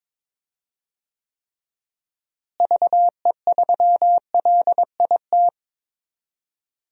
Morse code, heard as a keyed tone, about 22 wpm, 700 Hz. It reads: VE3LIT
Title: DXCC-100 Call Signs - 6 Characters - Repeated 3x - 22wpm